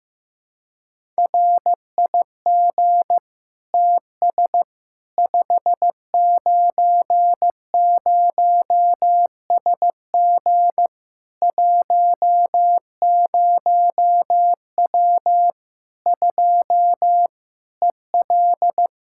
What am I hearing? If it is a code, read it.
Morse code, heard as a keyed tone, about 15 wpm, 700 Hz. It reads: RIG TS 590SG 10W 2 EL
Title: Practice QSO - Exchange 2 - Info - 15wpm